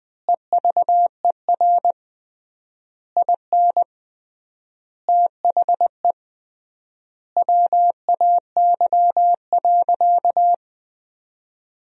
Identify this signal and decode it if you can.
Morse code, heard as a keyed tone, about 21 wpm, 700 Hz. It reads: EVER IN THE WAY.
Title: 3x Word-Spacing - Sentences from Top 400 Words - 20wpm